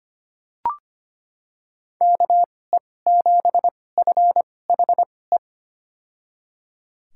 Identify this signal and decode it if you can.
Morse code, heard as a keyed tone, about 25 wpm, spaced as for 18 wpm, 700 Hz. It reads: KE7FHE
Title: USA Call Signs - 2x Character Spacing - 25wpm